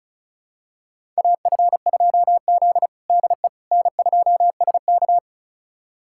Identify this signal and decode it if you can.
Morse code, heard as a keyed tone, about 36 wpm, 700 Hz. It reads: AF2Z DE N2SK
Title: USA Callsign DE Callsign - 4 Characters - 35wpm